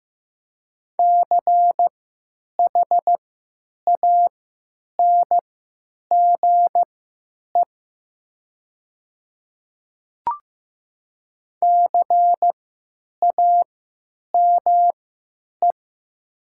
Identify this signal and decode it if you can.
Morse code, heard as a keyed tone, about 15 wpm, spaced as for 9 wpm, 700 Hz. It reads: CHANGE CAME
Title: 3x Character Spacing - Top 200 Words - 15wpm